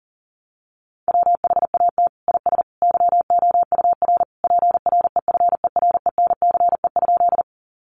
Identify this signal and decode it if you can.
Morse code, heard as a keyed tone, about 40 wpm, 700 Hz. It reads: WHAT IS YOUR PREFERENCE?
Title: Sentences from Top 1500 Words 40wpm